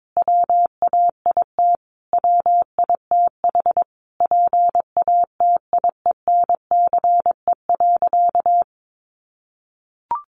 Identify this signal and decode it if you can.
Morse code, heard as a keyed tone, about 22 wpm, 700 Hz. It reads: WAIT WITH PATIENCE.
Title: Sentences from Top 1300 Words 22wpm